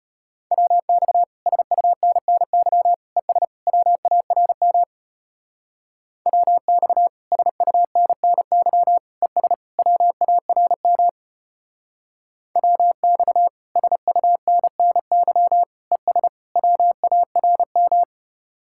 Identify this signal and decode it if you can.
Morse code, heard as a keyed tone, about 30 wpm, 700 Hz. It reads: WX SUNNY ES WARM WX SUNNY ES WARM WX SUNNY ES WARM
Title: Speed-Racing Common QSO Phrases Encore - 25wpm